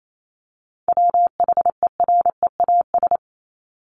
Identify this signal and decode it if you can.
Morse code, heard as a keyed tone, about 28 wpm, 700 Hz. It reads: WHEREAS
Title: Top 1501-1750 Words 28wpm